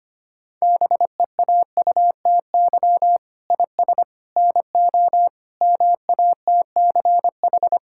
Morse code, heard as a keyed tone, about 25 wpm, 700 Hz. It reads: BEAUTY IS NO MATCH
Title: Sentences from Top 1600 Words 25wpm